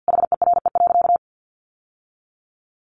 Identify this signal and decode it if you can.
Morse code, heard as a keyed tone, about 50 wpm, 700 Hz. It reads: HERE.
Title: Sentences from Top 1100 Words 50wpm